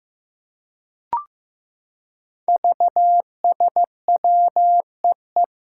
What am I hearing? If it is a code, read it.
Morse code, heard as a keyed tone, about 15 wpm, 700 Hz. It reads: VSWEE
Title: Five Challenging Letter-Number Combinations 15wpm